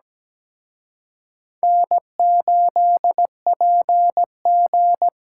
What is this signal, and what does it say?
Morse code, heard as a keyed tone, about 20 wpm, 700 Hz. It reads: N8PG
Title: Speed Racing - USA 1x2 Call Signs 17wpm